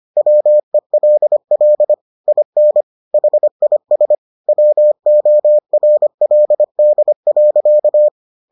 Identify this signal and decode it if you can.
Morse code, heard as a keyed tone, about 25 wpm, 600 Hz. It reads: WELL IN HIS WORLD.